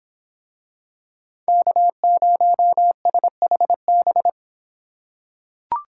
Morse code, heard as a keyed tone, about 25 wpm, 700 Hz. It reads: K0SHB